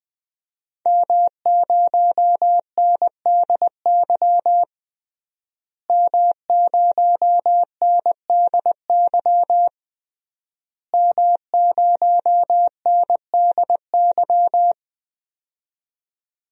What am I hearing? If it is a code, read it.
Morse code, heard as a keyed tone, about 20 wpm, 700 Hz. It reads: M0NDY M0NDY M0NDY
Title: DXCC-100 Call Signs - 5 Characters - Repeated 3x - 20wpm